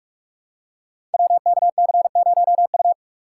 Morse code, heard as a keyed tone, about 44 wpm, 700 Hz. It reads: WKC0U
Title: Code Groups - 5 Letter:Numbers - 45wpm